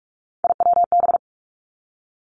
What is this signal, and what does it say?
Morse code, heard as a keyed tone, about 46 wpm, 700 Hz. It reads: IWB
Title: Three Challenging Letter-Number Combinations 45wpm